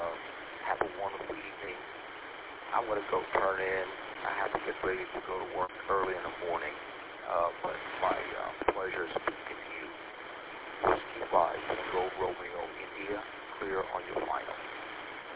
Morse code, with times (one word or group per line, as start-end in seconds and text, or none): none